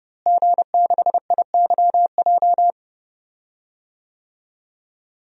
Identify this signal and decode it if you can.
Morse code, heard as a keyed tone, about 31 wpm, 700 Hz. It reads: G6IYJ
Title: DXCC-100 Call Signs - 5 Characters - Repeated 3x - 30wpm